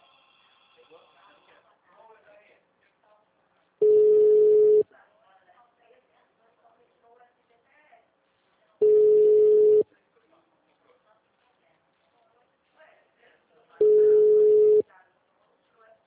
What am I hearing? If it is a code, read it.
Ringback tone (Europe, 425 Hz), with speech.
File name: out- -2021- - - .7822